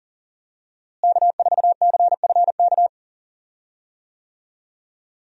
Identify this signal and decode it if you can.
Morse code, heard as a keyed tone, about 40 wpm, 700 Hz. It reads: K4CFK
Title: USA Call Signs - 5 Characters - Repeated 3x - 40wpm